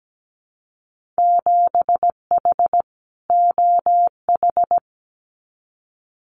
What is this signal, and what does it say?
Morse code, heard as a keyed tone, about 17 wpm, 700 Hz. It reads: 7H OH
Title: Winter Field Day Exchanges Encore 17wpm